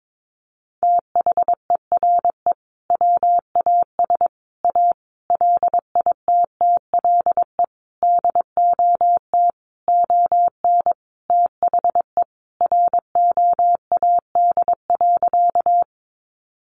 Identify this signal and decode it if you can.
Morse code, heard as a keyed tone, about 22 wpm, 700 Hz. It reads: THERE WAS A LITTLE DOT ON THE ROAD.
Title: Sentences from Top 1500 Words 22wpm